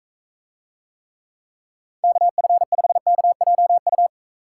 Morse code, heard as a keyed tone, about 43 wpm, 700 Hz. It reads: KF5KJU